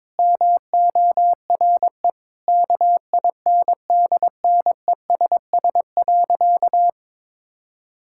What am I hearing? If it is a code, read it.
Morse code, heard as a keyed tone, about 22 wpm, 700 Hz. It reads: MORE KINDNESS.
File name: Sentences with 7-Word Phrases - 22wpm